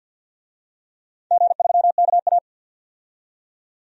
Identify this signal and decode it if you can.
Morse code, heard as a keyed tone, about 53 wpm, 700 Hz. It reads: G3KA
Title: DXCC-100 Call Signs - 4 Characters - 50wpm